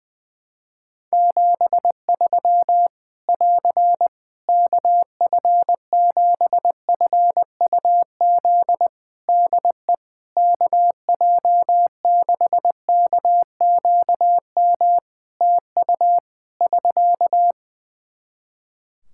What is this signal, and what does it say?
Morse code, heard as a keyed tone, about 20 wpm, 700 Hz. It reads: 73 <AR> KF7FUZ DE KJ6KQM TU <SK>
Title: Practice QSO - Exchange 4 - Ending - 20wpm